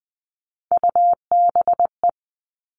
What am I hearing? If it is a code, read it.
Morse code, heard as a keyed tone, about 20 wpm, 700 Hz. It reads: UBE
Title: All Cut Numbers - 3-Digit 20wpm